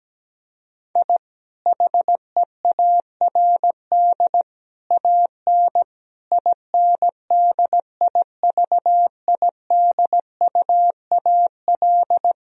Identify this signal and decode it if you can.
Morse code, heard as a keyed tone, about 17 wpm, 700 Hz. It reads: I HEARD AN INDIVIDUAL
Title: Sentences from Top 1000 Words - Review Part 4 - 17wpm